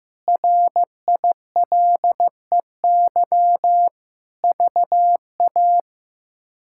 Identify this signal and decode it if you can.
Morse code, heard as a keyed tone, about 15 wpm, 700 Hz. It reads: RILEY VA
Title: K1USN SST QSO Practice - 15wpm